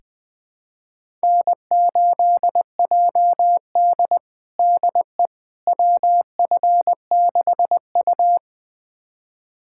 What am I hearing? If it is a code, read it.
Morse code, heard as a keyed tone, about 21 wpm, 700 Hz. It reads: N8JD DE WF6U
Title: USA Callsign DE Callsign - 4 Characters - 20wpm